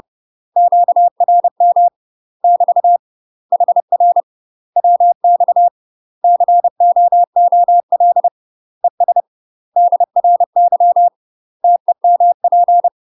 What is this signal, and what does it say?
Morse code, heard as a keyed tone, about 30 wpm, 700 Hz. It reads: QRM <BT> HR WX COOL ES DRY TEMP
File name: UK MASTER QSOs TRAINING / 30WPM / UK MASTER QSOs 0016